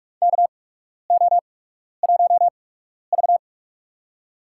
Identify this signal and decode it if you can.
Morse code, heard as a keyed tone, about 48 wpm, spaced as for 12 wpm, 700 Hz. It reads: KO1V
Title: USA Call Signs - 4 Characters - 8x Character Spacing - 45wpm